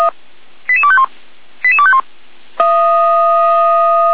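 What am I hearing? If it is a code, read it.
Five-tone selective call (ZVEI), heard as quick runs of five tones, tones of about 70 ms. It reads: 88251 88251
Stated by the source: ZVEI selective call received on an SDR.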